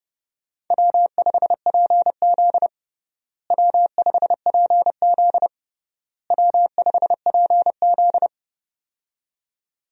Morse code, heard as a keyed tone, about 30 wpm, 700 Hz. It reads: W5PZ W5PZ W5PZ
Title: USA Call Signs - 4 Characters - Repeated 3x - 30wpm